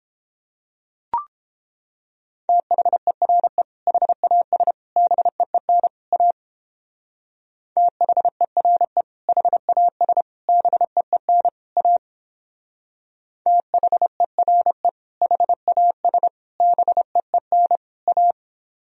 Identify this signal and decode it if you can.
Morse code, heard as a keyed tone, about 29 wpm, 700 Hz. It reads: THERE HAS BEEN A THERE HAS BEEN A THERE HAS BEEN A